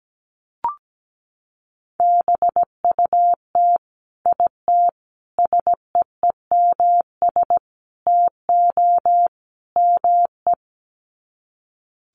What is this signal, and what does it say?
Morse code, heard as a keyed tone, about 17 wpm, 700 Hz. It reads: BUT IT SEEMS TO ME
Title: Sets of 5 Words Encore2 17wpm